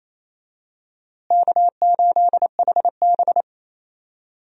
Morse code, heard as a keyed tone, about 28 wpm, 700 Hz. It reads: K8HB
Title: USA Call Signs - 4 Characters - Repeated 2x - 28wpm